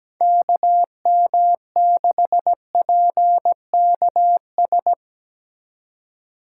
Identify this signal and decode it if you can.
Morse code, heard as a keyed tone, about 20 wpm, 700 Hz. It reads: KM6PKS
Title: Speed Racing - USA Call Signs 17wpm